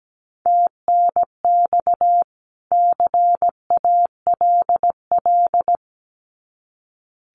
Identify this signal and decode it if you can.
Morse code, heard as a keyed tone, about 17 wpm, 700 Hz. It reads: TNX CALL
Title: Common QSO Phrases 17wpm